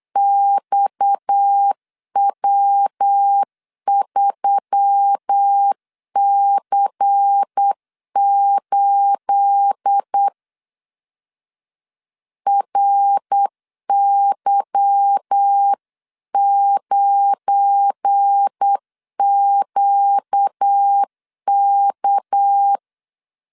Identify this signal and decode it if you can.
Morse code, heard as a keyed tone, about 8 wpm, 800 Hz. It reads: XW3C8 RY9QK
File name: CW / DET / 1 9 1500C S C 8wpm 5653